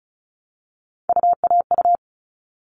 Morse code, heard as a keyed tone, about 34 wpm, 700 Hz. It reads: UAU
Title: All Cut Numbers - 3-Digit 35wpm